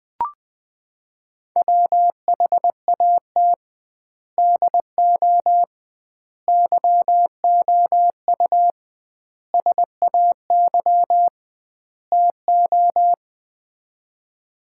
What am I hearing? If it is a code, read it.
Morse code, heard as a keyed tone, about 20 wpm, 700 Hz. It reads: WHAT DO YOU SAY TO
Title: 2x Word-Spacing - Sets of 5 Words - 20wpm